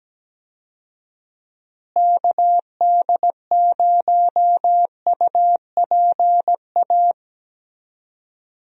Morse code, heard as a keyed tone, about 17 wpm, 700 Hz. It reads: KD0UPA
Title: USA Call Signs - 6 Characters - Repeated 2x - 17wpm